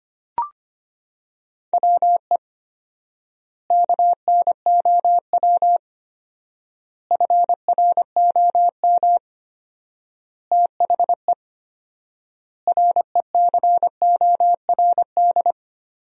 Morse code, heard as a keyed tone, about 25 wpm, 700 Hz. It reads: WE KNOW FROM THE RECORD
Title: 4x Word-Spacing - Sentences from Top 500 Words - 25wpm